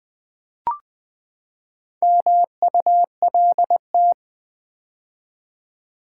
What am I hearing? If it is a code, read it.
Morse code, heard as a keyed tone, about 20 wpm, 700 Hz. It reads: MULT
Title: Build A Word - 4 Letters At A Time - 8-Letter Words - 20wpm